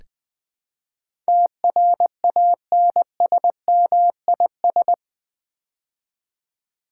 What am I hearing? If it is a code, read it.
Morse code, heard as a keyed tone, about 20 wpm, 700 Hz. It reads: TRANSMIS
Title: Build A Word - 4 Letters At A Time - 12-Letter Words - 20wpm